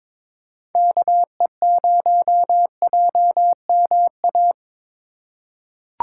Morse code, heard as a keyed tone, about 22 wpm, 700 Hz. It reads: KE0JMA
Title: USA Call Signs - 3x Character Spacing - 22wpm